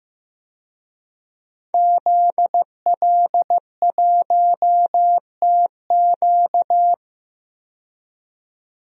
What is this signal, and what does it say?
Morse code, heard as a keyed tone, about 15 wpm, 700 Hz. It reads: ZL1TQ
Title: DXCC-100 Call Signs - 5 Characters - 15wpm